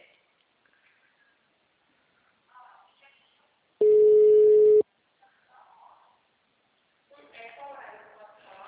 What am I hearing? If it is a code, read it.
Ringback tone (Europe, 425 Hz), with speech.
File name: out- -2016- - - .45